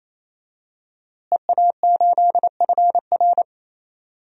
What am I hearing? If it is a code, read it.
Morse code, heard as a keyed tone, about 28 wpm, 700 Hz. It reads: EA8FR
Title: DXCC-100 Call Signs - 5 Characters - Repeated 3x - 28wpm